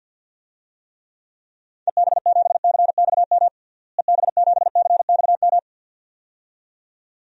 Morse code, heard as a keyed tone, about 49 wpm, 700 Hz. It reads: EB7CXM EB7CXM
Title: DXCC-100 Call Signs - 6 Characters - Repeated 2x - 50wpm